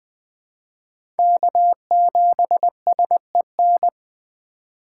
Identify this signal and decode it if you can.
Morse code, heard as a keyed tone, about 20 wpm, 700 Hz. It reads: K7SEN
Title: 2023 POTA Hunter Call Signs - 20wpm